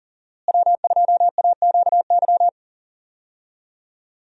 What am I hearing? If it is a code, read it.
Morse code, heard as a keyed tone, about 40 wpm, 700 Hz. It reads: W2AQY
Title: USA Call Signs - 5 Characters - Repeated 3x - 40wpm